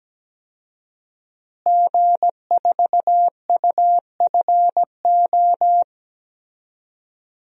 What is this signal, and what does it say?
Morse code, heard as a keyed tone, about 17 wpm, 700 Hz. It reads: G4UFO
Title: DXCC-100 Call Signs - 5 Characters - Repeated 2x - 17wpm